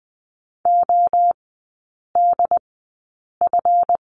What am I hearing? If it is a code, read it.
Morse code, heard as a keyed tone, about 20 wpm, 700 Hz. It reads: O D F
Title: 184 Lesson 30e - Letter b - Three Characters 20wpm